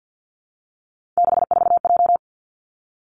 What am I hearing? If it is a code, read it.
Morse code, heard as a keyed tone, about 50 wpm, 700 Hz. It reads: 64J
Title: Three Challenging Letter-Number Combinations 50wpm